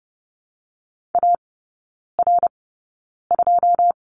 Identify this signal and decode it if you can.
Morse code, heard as a keyed tone, about 30 wpm, spaced as for 9 wpm, 700 Hz. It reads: AR2
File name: Lesson 16e - Letter c - Three Characters 30wpm